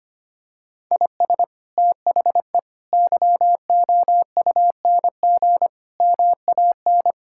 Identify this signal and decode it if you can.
Morse code, heard as a keyed tone, about 25 wpm, 700 Hz. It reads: IS THE YOUNG MAN
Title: Sentences from Top 1600 Words 25wpm